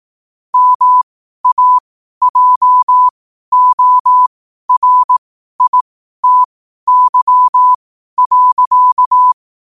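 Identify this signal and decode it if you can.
Morse code, heard as a keyed tone, about 18 wpm, spaced as for 12 wpm, 1000 Hz. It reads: MAJORITY.
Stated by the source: Morse code quote-of-the-day practice at 12 words per minute.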